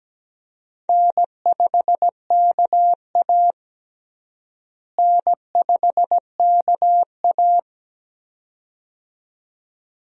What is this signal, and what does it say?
Morse code, heard as a keyed tone, about 17 wpm, 700 Hz. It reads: N5KA N5KA
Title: USA Call Signs - 4 Characters - Repeated 2x - 17wpm